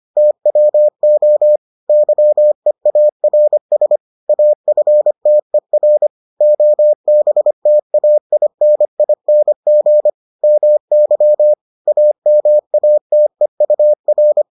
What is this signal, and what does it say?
Morse code, heard as a keyed tone, about 25 wpm, 600 Hz. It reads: TWO YEARS AFTER OBTAINING MY AMATEUR